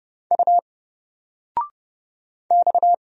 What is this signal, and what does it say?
Morse code, heard as a keyed tone, about 30 wpm, 700 Hz. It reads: U X
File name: Single-Letter-Number-30wpm